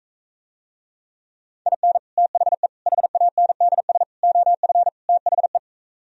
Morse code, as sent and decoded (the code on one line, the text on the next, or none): .. -. / - .... / .... .- -. -.. ... / --- ..-. / - .... .
IN THE HANDS OF THE